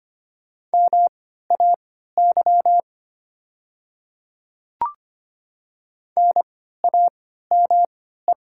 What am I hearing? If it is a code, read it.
Morse code, heard as a keyed tone, about 25 wpm, spaced as for 14 wpm, 700 Hz. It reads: MAY NAME
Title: 3x Character Spacing - Top 200 Words - 25wpm